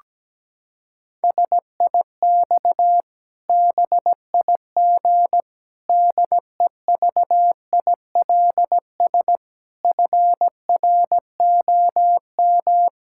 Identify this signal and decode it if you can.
Morse code, heard as a keyed tone, about 17 wpm, 700 Hz. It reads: SIX BIG DEVILS FROM
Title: Pangram Challenge v2 - 17wpm